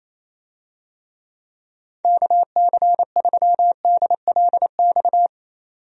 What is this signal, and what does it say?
Morse code, heard as a keyed tone, about 28 wpm, 700 Hz. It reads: KC3DLX
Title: USA Call Signs - 6 Characters - 28wpm